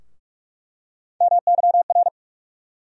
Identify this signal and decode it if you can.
Morse code, heard as a keyed tone, about 47 wpm, 700 Hz. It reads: MYR
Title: USA Call Sign Postfixes - 3 Letters 45wpm